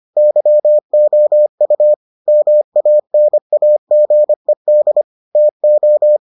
Morse code, heard as a keyed tone, about 25 wpm, 600 Hz. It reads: YOU MANAGED TO